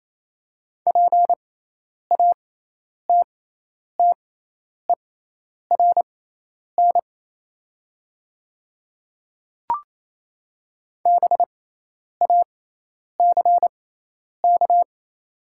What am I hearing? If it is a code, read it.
Morse code, heard as a keyed tone, about 28 wpm, spaced as for 10 wpm, 700 Hz. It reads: PATTERN BACK
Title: 6x Character Spacing - Top 500 Words - 28wpm